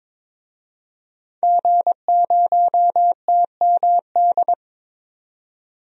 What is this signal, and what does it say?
Morse code, heard as a keyed tone, about 21 wpm, 700 Hz. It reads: G0TMD
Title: DXCC-100 Call Signs - 5 Characters - 22wpm